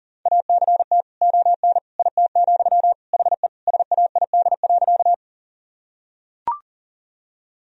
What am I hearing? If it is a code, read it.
Morse code, heard as a keyed tone, about 40 wpm, 700 Hz. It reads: ACT ON IT, HE SAID.
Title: Sentences from Top 1000 Words - Review Part 1 - 40wpm